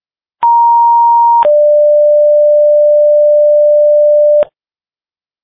Two-tone page, heard as a one-second tone, then a three-second tone, about 940 then 590 Hz. It is heard from 0.4 to 4.4 s.